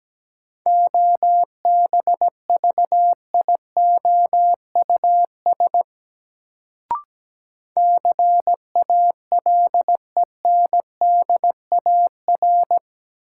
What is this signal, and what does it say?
Morse code, heard as a keyed tone, about 17 wpm, 700 Hz. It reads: OBVIOUS CALENDAR
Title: Sentences from Top 1600 Words - New Word Drill Alt - 17wpm